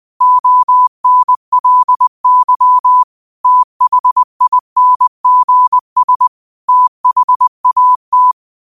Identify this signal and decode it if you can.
Morse code, heard as a keyed tone, about 20 wpm, 1000 Hz. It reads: ONLY THINGS THAT
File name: qotdfiles / 20wpm